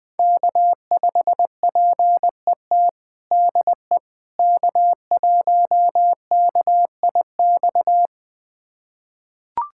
Morse code, heard as a keyed tone, about 20 wpm, 700 Hz. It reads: K5PET DE K1KIX
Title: USA Callsign DE Callsign - 5 Characters - 20wpm